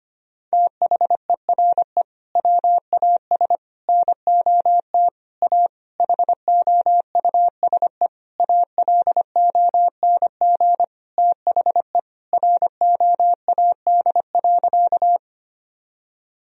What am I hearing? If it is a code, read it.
Morse code, heard as a keyed tone, about 25 wpm, 700 Hz. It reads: THERE WAS NOT A HOUSE ALONG THE ROAD.
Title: Sentences from Top 1000 Words - Review Part 2 - 25wpm